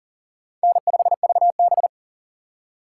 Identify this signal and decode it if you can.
Morse code, heard as a keyed tone, about 40 wpm, 700 Hz. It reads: N5VB
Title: USA Call Signs - 4 Characters - 3x Character Spacing - 40wpm